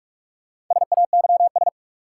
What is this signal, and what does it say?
Morse code, heard as a keyed tone, about 42 wpm, 700 Hz. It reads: SAYS
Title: Sets of 2 Words - New Word Drill - 45wpm